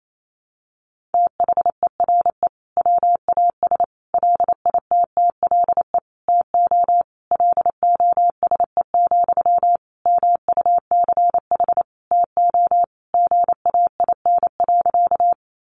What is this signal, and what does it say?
Morse code, heard as a keyed tone, about 28 wpm, 700 Hz. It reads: THERE WAS LITTLE TO LOSE, MUCH TO GAIN.
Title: Sentences from Top 1100 Words 28wpm